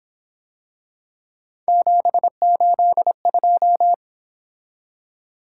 Morse code, heard as a keyed tone, about 26 wpm, 700 Hz. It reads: 782
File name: Speed-Racing - Three Letter-Number Combinations - 22wpm